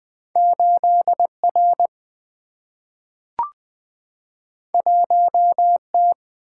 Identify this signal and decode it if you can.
Morse code, heard as a keyed tone, about 20 wpm, 700 Hz. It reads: 8R 1T
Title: Single-Letter Single-Number 20wpm